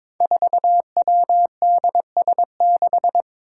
Morse code, heard as a keyed tone, about 23 wpm, 700 Hz. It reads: 4WDS6
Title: Five Challenging Letter-Number Combinations 22wpm